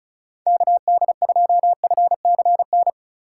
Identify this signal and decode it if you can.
Morse code, heard as a keyed tone, about 36 wpm, 700 Hz. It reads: KD2FCN
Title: USA Call Signs - 6 Characters - 4x Character Spacing - 35wpm